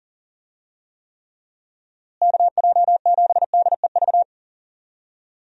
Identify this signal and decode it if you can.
Morse code, heard as a keyed tone, about 40 wpm, 700 Hz. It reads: KJ7DEV
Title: USA Call Signs - 6 Characters - Repeated 2x - 40wpm